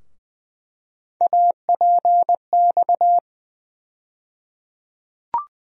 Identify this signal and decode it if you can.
Morse code, heard as a keyed tone, about 20 wpm, 700 Hz. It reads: APX
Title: USA Call Sign Postfixes - 3 Letters 20wpm